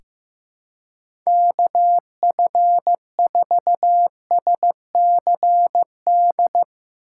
Morse code, heard as a keyed tone, about 15 wpm, 700 Hz. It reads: KF4SCD
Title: USA Call Signs - 4x Character Spacing - 15wpm